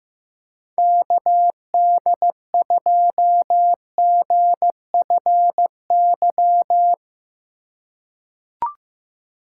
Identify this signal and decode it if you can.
Morse code, heard as a keyed tone, about 16 wpm, 700 Hz. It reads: KD2GFY
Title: USA Call Signs - 6 Characters - 2x Character Spacing - 15wpm